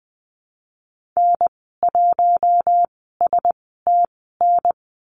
Morse code, heard as a keyed tone, about 20 wpm, spaced as for 14 wpm, 700 Hz. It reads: N1STN
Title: 154 Lesson 25g - Letter p - Call Signs 20wpm